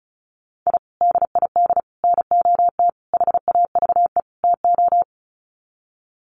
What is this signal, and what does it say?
Morse code, heard as a keyed tone, about 34 wpm, 700 Hz. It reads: I DID NOT HAVE TO
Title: Sets of 5 Words 35wpm